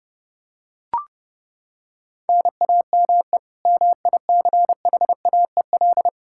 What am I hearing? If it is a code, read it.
Morse code, heard as a keyed tone, about 30 wpm, 700 Hz. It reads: NAME MICHAEL
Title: Practice QSO - Exchange 1 - Beginning - 30wpm